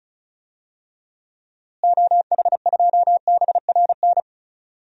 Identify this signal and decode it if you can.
Morse code, heard as a keyed tone, about 35 wpm, 700 Hz. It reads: OH2BRN
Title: DXCC-100 Call Signs - 6 Characters - Repeated 3x - 35wpm